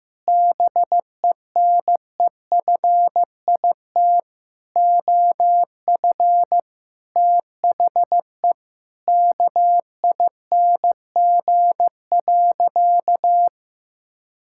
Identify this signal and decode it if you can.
Morse code, heard as a keyed tone, about 15 wpm, 700 Hz. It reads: BENEFIT OF THE KING.
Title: Sentences from Top 1000 Words - Review Part 6 - 15wpm